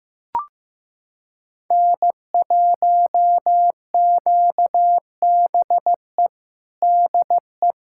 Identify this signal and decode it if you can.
Morse code, heard as a keyed tone, about 16 wpm, 700 Hz. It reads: N1QBE DE
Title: USA Callsign DE Callsign - 5 Characters - 15wpm